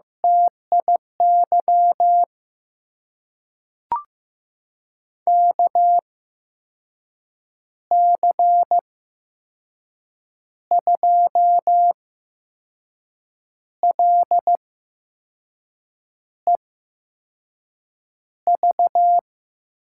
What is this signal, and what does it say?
Morse code, heard as a keyed tone, about 15 wpm, spaced as for 5 wpm, 700 Hz. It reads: TIY KC2LEV
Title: USA Call Signs - 6 Characters - 8x Character Spacing - 15wpm